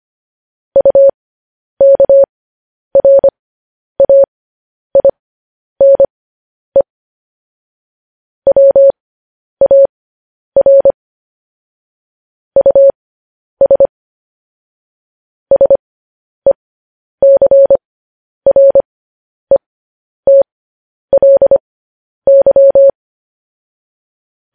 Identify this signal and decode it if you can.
Morse code, heard as a keyed tone, about 25 wpm, spaced as for 10 wpm, 550 Hz. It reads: UKRAINE WAR US SECRETLY